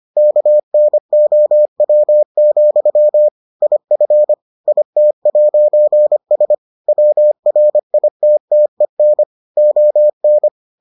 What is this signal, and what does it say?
Morse code, heard as a keyed tone, about 25 wpm, 600 Hz. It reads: KNOW, IF IT'S WRITTEN ON